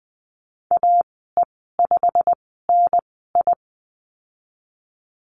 Morse code, heard as a keyed tone, about 20 wpm, spaced as for 15 wpm, 700 Hz. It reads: AE5NI
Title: 088 Lesson 15g - Number 5 - Call Signs 20wpm